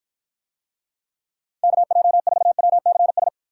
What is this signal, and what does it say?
Morse code, heard as a keyed tone, about 54 wpm, 700 Hz. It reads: KJ4WCS